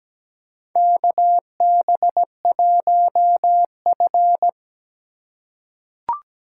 Morse code, heard as a keyed tone, about 17 wpm, 700 Hz. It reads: KB1F
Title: USA Call Signs - 4 Characters - 4x Character Spacing - 17wpm